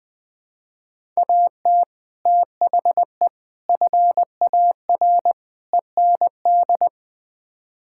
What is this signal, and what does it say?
Morse code, heard as a keyed tone, about 20 wpm, 700 Hz. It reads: AT THE FAR END
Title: Build A Phrase - 5-Word Phrases - 20wpm